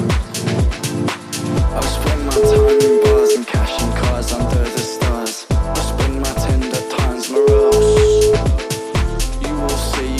Ringback tone (Europe, 425 Hz), with noise.